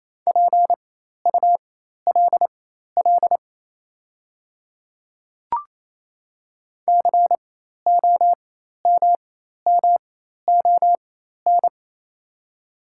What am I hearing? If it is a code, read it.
Morse code, heard as a keyed tone, about 27 wpm, spaced as for 13 wpm, 690 Hz. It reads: PULL COMMON
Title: 4x Character Spacing - Top 500 Words - 28wpm